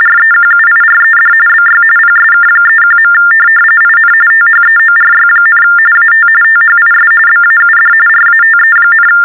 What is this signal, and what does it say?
TTY/TDD text telephone tones, about 45.45 baud. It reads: REQUIRED BY PSK31. IN THEORY, AT THIS BAUD RATE,
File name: teletype-RTTY-45